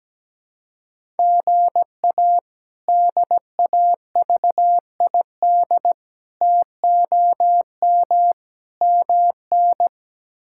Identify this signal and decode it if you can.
Morse code, heard as a keyed tone, about 17 wpm, 700 Hz. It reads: GA DAVID TOM MN
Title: K1USN SST QSO Practice - 17wpm